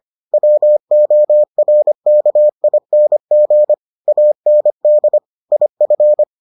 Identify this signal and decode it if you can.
Morse code, heard as a keyed tone, about 25 wpm, 600 Hz. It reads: WORKING AND IF